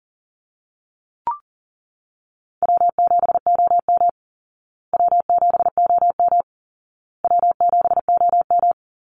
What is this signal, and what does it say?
Morse code, heard as a keyed tone, about 40 wpm, 700 Hz. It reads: W7OM W7OM W7OM
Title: Easy Call Signs Repeated 3x - 40wpm